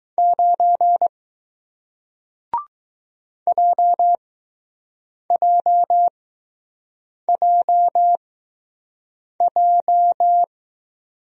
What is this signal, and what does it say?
Morse code, heard as a keyed tone, about 20 wpm, spaced as for 6 wpm, 700 Hz. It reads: 9 JJJJ